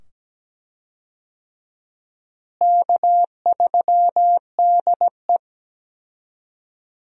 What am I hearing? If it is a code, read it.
Morse code, heard as a keyed tone, about 17 wpm, 700 Hz. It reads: K3DE
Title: USA Call Signs - 4 Characters - Repeated 2x - 17wpm